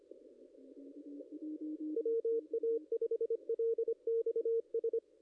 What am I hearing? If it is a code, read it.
Morse code, heard as a keyed tone, about 25 wpm, 460 Hz, with radio noise. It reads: WA5LXS
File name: contests / ss-cw-k5tr-op-n5rz -0200